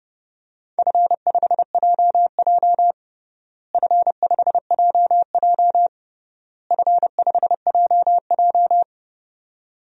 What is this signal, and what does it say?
Morse code, heard as a keyed tone, about 32 wpm, 700 Hz. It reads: F5JJ F5JJ F5JJ
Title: DXCC-100 Call Signs - 4 Characters - Repeated 3x - 30wpm